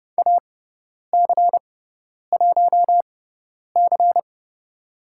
Morse code, heard as a keyed tone, about 30 wpm, spaced as for 10 wpm, 700 Hz. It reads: AC1C
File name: Lesson 16g - Letter c - Call Signs 30wpm